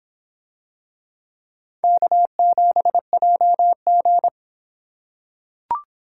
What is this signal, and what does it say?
Morse code, heard as a keyed tone, about 26 wpm, 700 Hz. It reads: K7JG